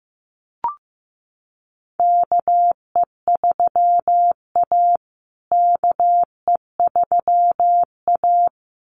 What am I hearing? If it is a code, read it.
Morse code, heard as a keyed tone, about 15 wpm, 700 Hz. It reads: KE3A KE3A
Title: USA Call Signs Repeated 2x 15wpm